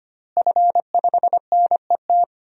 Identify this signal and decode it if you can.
Morse code, heard as a keyed tone, about 25 wpm, 700 Hz. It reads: F5NET
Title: DXCC-100 Call Signs - 5 Characters - Repeated 3x - 25wpm